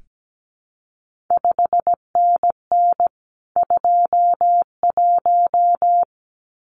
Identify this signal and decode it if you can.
Morse code, heard as a keyed tone, about 17 wpm, 700 Hz. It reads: 5NN 21
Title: CQ WW DX Contest Exchange Encore - 17wpm